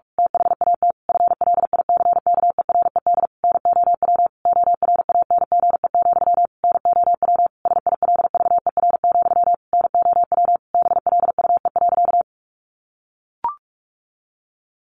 Morse code, heard as a keyed tone, about 44 wpm, 700 Hz. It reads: THAT FLICKERED NOW ORANGE, NOW SILVER, NOW BLUE.
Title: A Sound Of Thunder 45wpm